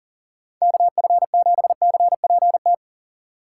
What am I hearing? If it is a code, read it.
Morse code, heard as a keyed tone, about 51 wpm, 700 Hz. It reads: KF7CPT